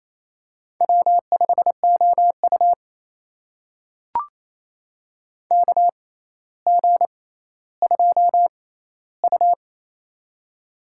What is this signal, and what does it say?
Morse code, heard as a keyed tone, about 27 wpm, spaced as for 14 wpm, 700 Hz. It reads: W5OU KG2U
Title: USA Call Signs - 4 Characters - 6x Character Spacing - 28wpm